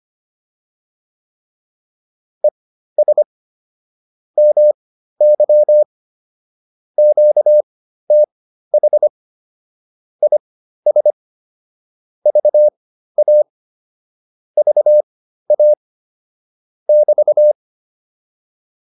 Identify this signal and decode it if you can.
Morse code, heard as a keyed tone, about 25 wpm, spaced as for 13 wpm, 600 Hz. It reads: ES MY QTH IS VA VA <BT>